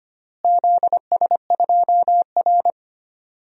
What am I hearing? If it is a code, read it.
Morse code, heard as a keyed tone, about 25 wpm, 700 Hz. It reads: ZS2R
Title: DXCC-100 Call Signs - 4 Characters - Repeated 3x - 25wpm